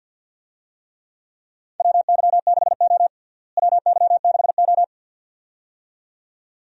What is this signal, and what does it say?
Morse code, heard as a keyed tone, about 50 wpm, 700 Hz. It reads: WY6O WY6O
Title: USA Call Signs - 4 Characters - Repeated 2x - 50wpm